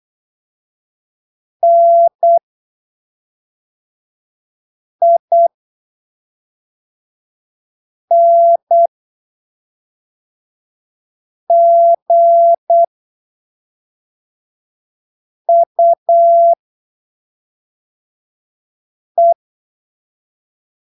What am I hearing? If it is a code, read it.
Morse code, heard as a keyed tone, about 8 wpm, spaced as for 3 wpm, 680 Hz. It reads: NINGUE
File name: arquivos / texto / texto 28 08ppm